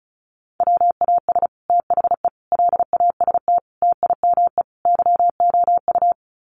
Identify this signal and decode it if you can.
Morse code, heard as a keyed tone, about 34 wpm, 700 Hz. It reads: WAS THE LAST TIME YOU
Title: Sets of 5 Words Encore2 35wpm